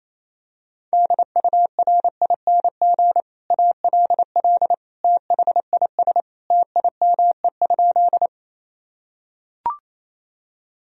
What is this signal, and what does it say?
Morse code, heard as a keyed tone, about 28 wpm, 700 Hz. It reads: DURING ALL THIS TIME?
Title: Sentences from Top 1000 Words - Review Part 2 - 28wpm